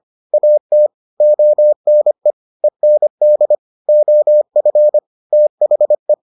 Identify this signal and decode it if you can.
Morse code, heard as a keyed tone, about 25 wpm, 600 Hz. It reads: AT ONE END OF THE